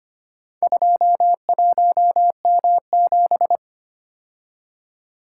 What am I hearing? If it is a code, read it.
Morse code, heard as a keyed tone, about 25 wpm, 700 Hz. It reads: 21M7
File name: practice / 25wpm / Four Letter-Number Combinations - 25wpm